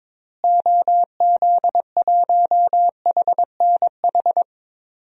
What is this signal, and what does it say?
Morse code, heard as a keyed tone, about 21 wpm, 700 Hz. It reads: OZ1HNH
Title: DXCC-100 Call Signs - 6 Characters - Repeated 3x - 22wpm